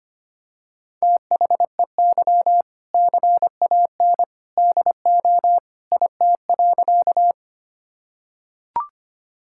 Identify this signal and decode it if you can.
Morse code, heard as a keyed tone, about 25 wpm, 700 Hz. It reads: THEY CAN DO IT.
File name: practice / 25wpm / Sentences with 2-Word Phrases - 25wpm